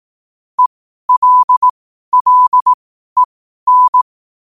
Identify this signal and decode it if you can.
Morse code, heard as a keyed tone, about 18 wpm, spaced as for 12 wpm, 1000 Hz. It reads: ELLEN
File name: qotdfiles / 12wpm